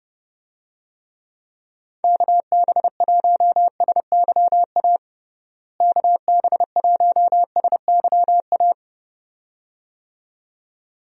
Morse code, heard as a keyed tone, about 30 wpm, 700 Hz. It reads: KB1SYA KB1SYA
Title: USA Call Signs - 6 Characters - Repeated 2x - 30wpm